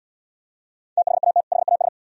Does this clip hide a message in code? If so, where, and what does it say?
Yes: Morse code from 1.0 to 1.9 s, heard as a keyed tone, about 78 wpm, 700 Hz. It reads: THAT HAS